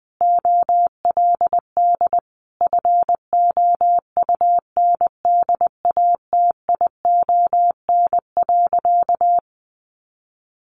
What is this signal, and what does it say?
Morse code, heard as a keyed tone, about 20 wpm, 700 Hz. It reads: OLD FOUNDATION.
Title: Sentences from Top 1200 Words 20wpm